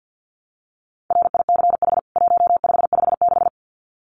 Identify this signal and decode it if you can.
Morse code, heard as a keyed tone, about 50 wpm, 700 Hz. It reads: RICH 1556
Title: CWT Exchange 50wpm